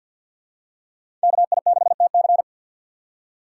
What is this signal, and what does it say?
Morse code, heard as a keyed tone, about 51 wpm, 700 Hz. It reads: KI6TC
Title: USA Call Signs - 4x Character Spacing - 50wpm